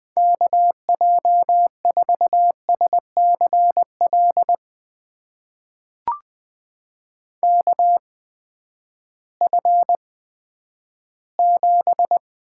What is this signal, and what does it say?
Morse code, heard as a keyed tone, about 20 wpm, spaced as for 11 wpm, 700 Hz. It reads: KJ4SCL KF7
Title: USA Call Signs - 6 Characters - 8x Character Spacing - 20wpm